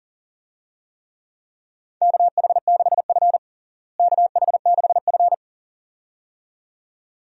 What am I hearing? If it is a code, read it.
Morse code, heard as a keyed tone, about 40 wpm, 700 Hz. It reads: KH6F KH6F
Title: USA Call Signs - 4 Characters - Repeated 2x - 40wpm